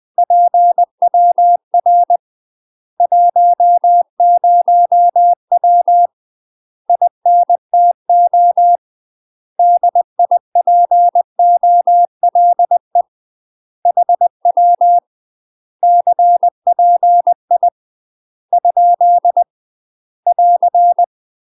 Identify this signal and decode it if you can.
Morse code, heard as a keyed tone, about 20 wpm, 700 Hz. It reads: PWR 10W INTO DIPOLE HW CPI ? <AR>
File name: UK MASTER QSOs TRAINING / 20WPM / UK MASTER QSOs 0059